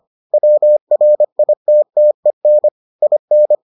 Morse code, heard as a keyed tone, about 25 wpm, 600 Hz. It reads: WRITTEN IN